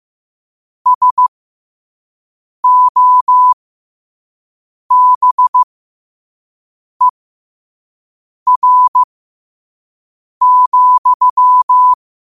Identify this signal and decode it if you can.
Morse code, heard as a keyed tone, about 15 wpm, spaced as for 5 wpm, 1000 Hz. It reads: SOBER,
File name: qotdfiles / 5wpm